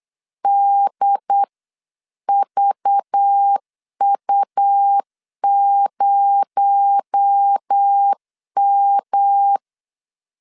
Morse code, heard as a keyed tone, about 8 wpm, 800 Hz. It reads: DVU0M
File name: LEZIONI / 7 GF9Z / 7 GF9 8wpm 3545